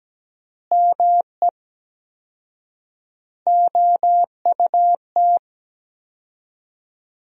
Morse code, heard as a keyed tone, about 17 wpm, 700 Hz. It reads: ME OUT
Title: 4x Word-Spacing - Sentences from Top 400 Words - 17wpm